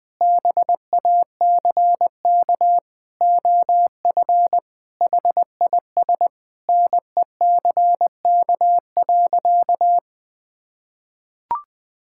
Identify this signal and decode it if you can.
Morse code, heard as a keyed tone, about 20 wpm, 700 Hz. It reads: BACK OF HIS NECK.
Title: Sentences with 4-Word Phrases - 20wpm